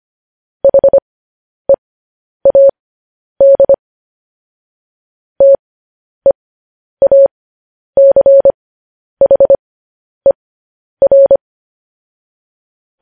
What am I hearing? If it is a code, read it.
Morse code, heard as a keyed tone, about 25 wpm, spaced as for 10 wpm, 550 Hz. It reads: HEAD TEACHER
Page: News Headlines in Morse Code at 10 WPM